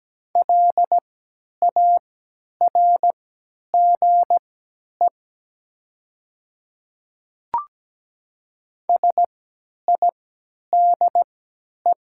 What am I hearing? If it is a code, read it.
Morse code, heard as a keyed tone, about 17 wpm, spaced as for 10 wpm, 700 Hz. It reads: LARGE SIDE
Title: 3x Character Spacing - Top 200 Words - 17wpm